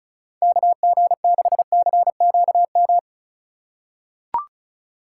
Morse code, heard as a keyed tone, about 36 wpm, 700 Hz. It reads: KG6CQM
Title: USA Call Signs - 6 Characters - 3x Character Spacing - 35wpm